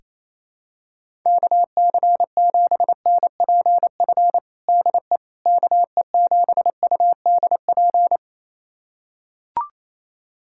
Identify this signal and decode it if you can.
Morse code, heard as a keyed tone, about 28 wpm, 700 Hz. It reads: KC7NPF DE KE7UDP
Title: USA Callsign DE Callsign - 6 Characters - 28wpm